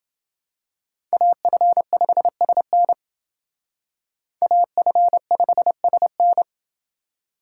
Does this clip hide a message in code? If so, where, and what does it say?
Yes: Morse code from 1.1 to 6.4 s, heard as a keyed tone, about 28 wpm, 700 Hz. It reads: AF5SN AF5SN